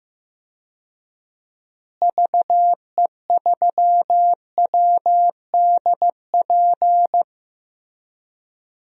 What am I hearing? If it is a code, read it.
Morse code, heard as a keyed tone, about 15 wpm, 700 Hz. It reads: VE3WDP
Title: DXCC-100 Call Signs - 6 Characters - Repeated 3x - 15wpm